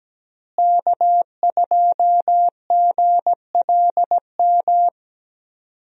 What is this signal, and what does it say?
Morse code, heard as a keyed tone, about 17 wpm, 700 Hz. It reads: K2GLM
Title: USA Call Signs - 5 Characters - Repeated 3x - 17wpm